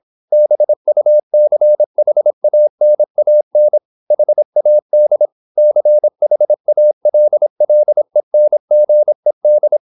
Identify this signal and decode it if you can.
Morse code, heard as a keyed tone, about 26 wpm, 600 Hz. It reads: BUCHANAN HAD CHALLENGED
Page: Never 26wpm | CW med Gnister